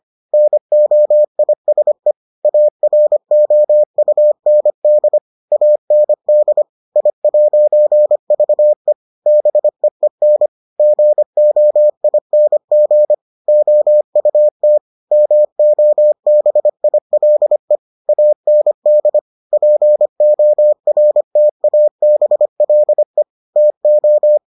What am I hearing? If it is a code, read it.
Morse code, heard as a keyed tone, about 25 wpm, 600 Hz. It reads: NOISE AROUND AND I'VE BEEN GOING OUT MOBILE AND PORTABLE TO